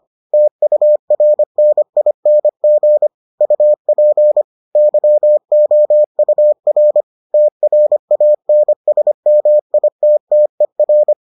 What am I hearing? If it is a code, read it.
Morse code, heard as a keyed tone, about 25 wpm, 600 Hz. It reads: TURNING UP YOUR TRANSMITTER